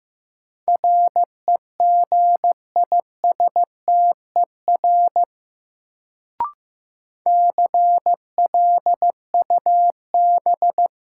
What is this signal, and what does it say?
Morse code, heard as a keyed tone, about 15 wpm, 700 Hz. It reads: REGISTER CLUB
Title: Sentences from Top 1200 Words - New Word Drill - 15wpm